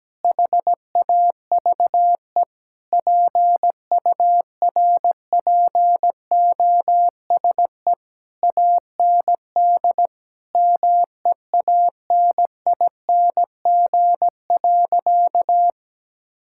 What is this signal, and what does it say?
Morse code, heard as a keyed tone, about 17 wpm, 700 Hz. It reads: HAVE PURPOSE AND MEANING.
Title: Sentences from Top 1000 Words - Review Part 6 - 17wpm